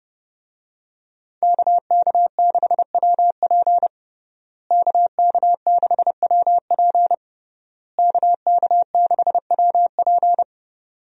Morse code, heard as a keyed tone, about 29 wpm, 700 Hz. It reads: KK6WP KK6WP KK6WP
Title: USA Call Signs - 5 Characters - Repeated 3x - 30wpm